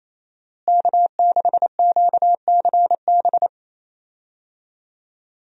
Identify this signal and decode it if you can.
Morse code, heard as a keyed tone, about 27 wpm, 700 Hz. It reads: K6QCB
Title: USA Call Signs - 5 Characters - Repeated 2x - 28wpm